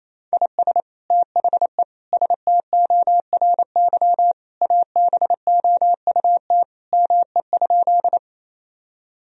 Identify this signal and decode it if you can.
Morse code, heard as a keyed tone, about 28 wpm, 700 Hz. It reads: IS THE STORY ABOUT ME?
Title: Sentences from Top 1000 Words - Review Part 1 - 28wpm